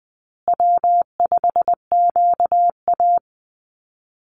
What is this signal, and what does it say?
Morse code, heard as a keyed tone, about 20 wpm, 700 Hz. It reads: W5QA
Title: USA Call Signs Repeated 2x 20wpm